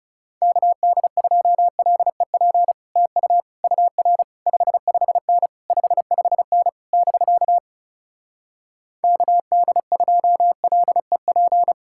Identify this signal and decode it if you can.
Morse code, heard as a keyed tone, about 40 wpm, 700 Hz. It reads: KD2LEP TU UR 55N 55N <BK> KD2LEP